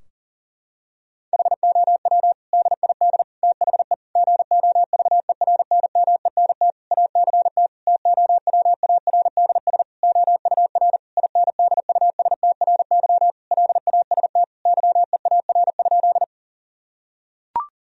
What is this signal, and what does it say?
Morse code, heard as a keyed tone, about 37 wpm, 700 Hz. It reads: HOW DID THE GOVERNMENT ACT TOWARDS OUR INDUSTRY LAST YEAR?
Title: Sentences from Top 1000 Words - Review Part 4 - 40wpm